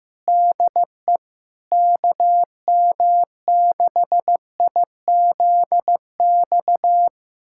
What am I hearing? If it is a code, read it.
Morse code, heard as a keyed tone, about 15 wpm, 700 Hz. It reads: DE KM6IZX
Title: USA Callsign DE Callsign - 6 Characters - 15wpm